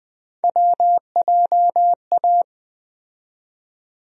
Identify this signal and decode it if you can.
Morse code, heard as a keyed tone, about 20 wpm, 700 Hz. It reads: WJA
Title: USA Call Sign Postfixes - 3 Letters 20wpm